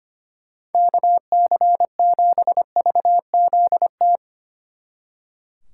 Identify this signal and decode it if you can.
Morse code, heard as a keyed tone, about 32 wpm, 700 Hz. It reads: KC7VZT